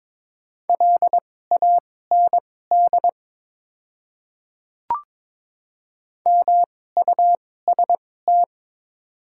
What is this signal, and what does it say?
Morse code, heard as a keyed tone, about 22 wpm, spaced as for 16 wpm, 700 Hz. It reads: LAND MUST
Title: 2x Character Spacing - Top 500 Words - 22wpm